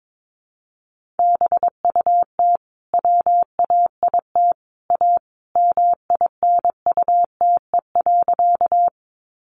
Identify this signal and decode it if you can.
Morse code, heard as a keyed tone, about 22 wpm, 700 Hz. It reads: BUT WAIT A MINUTE.
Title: Sentences from Top 500 Words 22wpm